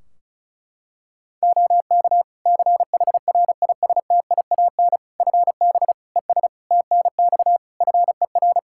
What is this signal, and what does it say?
Morse code, heard as a keyed tone, about 36 wpm, 700 Hz. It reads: OK CHRISTIAN FB ES TNX FER
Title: Practice QSO - Exchange 2 - Info - 35wpm